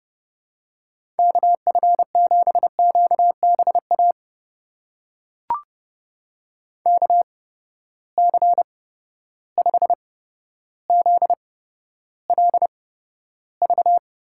Morse code, heard as a keyed tone, about 32 wpm, spaced as for 13 wpm, 700 Hz. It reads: KF7QBA KC5ZLV
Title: USA Call Signs - 6 Characters - 8x Character Spacing - 30wpm